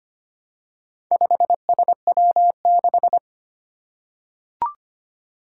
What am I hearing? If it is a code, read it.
Morse code, heard as a keyed tone, about 25 wpm, 700 Hz. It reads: 5SW6
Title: Four Challenging Letter-Number Combinations 25wpm